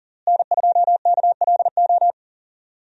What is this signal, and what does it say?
Morse code, heard as a keyed tone, about 40 wpm, 700 Hz. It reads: N2KLO
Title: USA Call Signs - 5 Characters - Repeated 3x - 40wpm